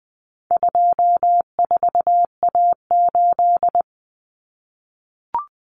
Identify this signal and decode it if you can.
Morse code, heard as a keyed tone, about 19 wpm, 700 Hz. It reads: 24A8
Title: Common Cut Numbers - 4-Digit 20wpm